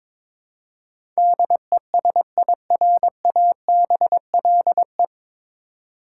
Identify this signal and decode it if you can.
Morse code, heard as a keyed tone, about 22 wpm, 700 Hz. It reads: DESIRABLE
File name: 9 Letter Words - First 3 Letters Given - 22wpm